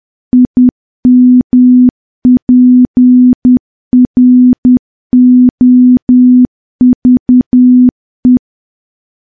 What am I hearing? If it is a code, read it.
Morse code, heard as a keyed tone, about 10 wpm, 250 Hz. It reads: IMPROVE
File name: morse / morse-training-10WPM